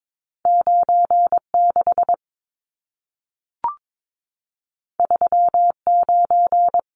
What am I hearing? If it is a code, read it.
Morse code, heard as a keyed tone, about 22 wpm, 700 Hz. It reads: 96 39
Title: Numbers - 2-Digit 22wpm